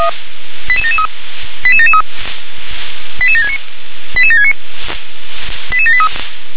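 Five-tone selective call (ZVEI), heard as quick runs of five tones, tones of about 70 ms. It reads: 88773 88773 88769 88769 88783
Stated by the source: ZVEI five-tone selective call received on an SDR.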